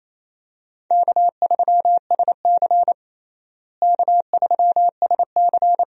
Morse code, heard as a keyed tone, about 28 wpm, 700 Hz. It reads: K3SC K3SC
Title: USA Call Signs - 4 Characters - Repeated 2x - 28wpm